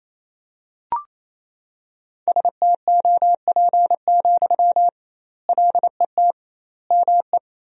Morse code, heard as a keyed tone, about 28 wpm, 700 Hz. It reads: STOP, LET ME
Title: 2x Word-Spacing - Sentences from Top 300 Words - 28wpm